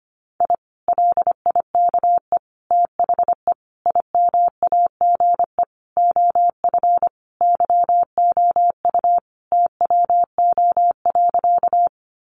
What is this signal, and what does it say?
Morse code, heard as a keyed tone, about 25 wpm, 700 Hz. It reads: I LIKE THE IMAGE OF YOU TWO.
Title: Sentences from Top 700 Words 25wpm